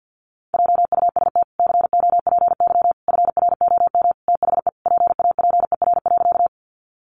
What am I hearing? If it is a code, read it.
Morse code, heard as a keyed tone, about 50 wpm, 700 Hz. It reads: JUST COPY FROM THE PAPER.
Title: Sentences from Top 1400 Words 50wpm